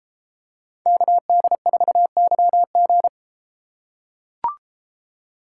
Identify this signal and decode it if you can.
Morse code, heard as a keyed tone, about 30 wpm, 700 Hz. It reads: KD4YG